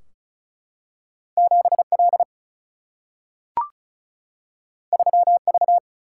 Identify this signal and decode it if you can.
Morse code, heard as a keyed tone, about 36 wpm, 700 Hz. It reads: 7 L 3 V
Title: Single-Letter Single-Number 35wpm